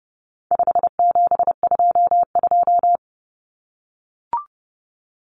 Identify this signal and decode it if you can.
Morse code, heard as a keyed tone, about 30 wpm, 700 Hz. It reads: 5722
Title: Common Cut Numbers - 4-Digit 30wpm